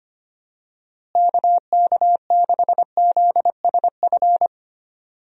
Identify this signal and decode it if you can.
Morse code, heard as a keyed tone, about 25 wpm, 700 Hz. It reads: KK6ZSF
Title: USA Call Signs - 6 Characters - Repeated 2x - 25wpm